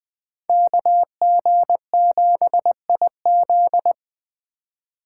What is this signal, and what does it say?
Morse code, heard as a keyed tone, about 21 wpm, 700 Hz. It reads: KG7IZ